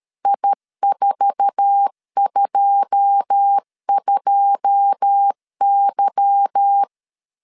Morse code, heard as a keyed tone, about 13 wpm, 800 Hz. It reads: I422Y
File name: LEZIONI / 9 2 12wpm 2301